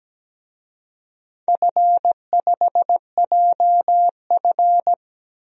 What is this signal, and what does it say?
Morse code, heard as a keyed tone, about 17 wpm, 700 Hz. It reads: F5JF
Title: DXCC-100 Call Signs - 4 Characters - Repeated 3x - 17wpm